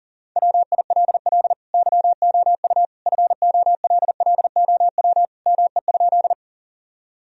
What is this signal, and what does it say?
Morse code, heard as a keyed tone, about 40 wpm, 700 Hz. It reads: WILL YOU FOLLOW ME?
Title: Sentences from Top 1000 Words - Review Part 1 - 40wpm